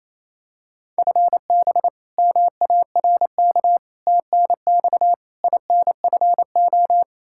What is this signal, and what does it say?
Morse code, heard as a keyed tone, about 28 wpm, 700 Hz. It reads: FB MARK TNX INFO
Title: K1USN SST QSO Practice 28wpm